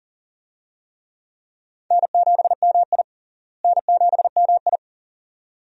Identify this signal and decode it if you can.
Morse code, heard as a keyed tone, about 40 wpm, 700 Hz. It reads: N7MI N7MI
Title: USA Call Signs - 4 Characters - Repeated 2x - 40wpm